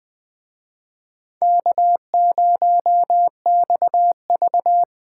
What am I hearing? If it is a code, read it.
Morse code, heard as a keyed tone, about 20 wpm, 700 Hz. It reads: K0XV
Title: Speed Racing - USA 1x2 Call Signs 17wpm